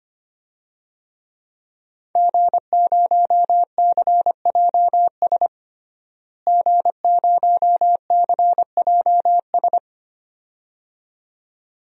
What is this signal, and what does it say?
Morse code, heard as a keyed tone, about 25 wpm, 700 Hz. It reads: G0CJS G0CJS
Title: DXCC-100 Call Signs - 5 Characters - Repeated 2x - 25wpm